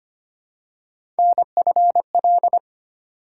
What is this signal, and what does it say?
Morse code, heard as a keyed tone, about 25 wpm, 700 Hz. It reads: NFL
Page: morse-code-ninja-practice-25wpm